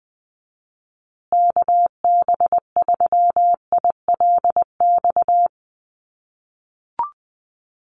Morse code, heard as a keyed tone, about 20 wpm, 700 Hz. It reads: KB3ILX
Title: USA Call Signs Repeated 3x 20wpm